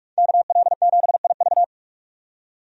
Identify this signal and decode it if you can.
Morse code, heard as a keyed tone, about 46 wpm, 700 Hz. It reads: KL7IV
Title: DXCC-100 Call Signs - 5 Characters - 45wpm